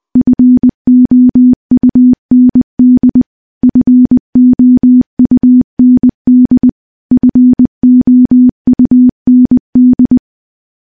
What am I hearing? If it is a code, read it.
Morse code, heard as a keyed tone, about 20 wpm, 250 Hz. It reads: FOUND FOUND FOUND
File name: morse-training-20WPM